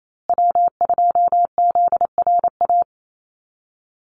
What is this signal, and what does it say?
Morse code, heard as a keyed tone, about 28 wpm, 700 Hz. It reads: W2ZRA
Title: Call Signs Repeated 2x - 28wpm